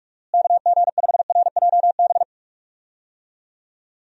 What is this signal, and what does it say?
Morse code, heard as a keyed tone, about 46 wpm, 700 Hz. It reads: KG5RJB
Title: USA Call Signs - 6 Characters - Repeated 3x - 45wpm